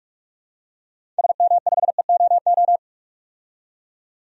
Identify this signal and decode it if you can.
Morse code, heard as a keyed tone, about 42 wpm, 700 Hz. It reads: SM5EOO
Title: DXCC-100 Call Signs - 6 Characters - 45wpm